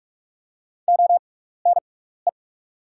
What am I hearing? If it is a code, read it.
Morse code, heard as a keyed tone, about 44 wpm, spaced as for 15 wpm, 700 Hz. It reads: ONE
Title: 6x Character Spacing - Top 100 Words - 45wpm